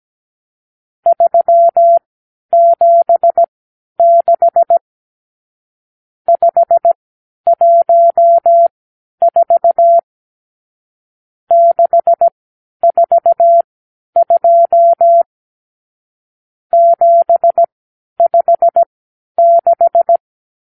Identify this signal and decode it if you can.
Morse code, heard as a keyed tone, about 17 wpm, spaced as for 10 wpm, 680 Hz. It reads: 376 514 642 756
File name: k7qo / Lesson 52 (8)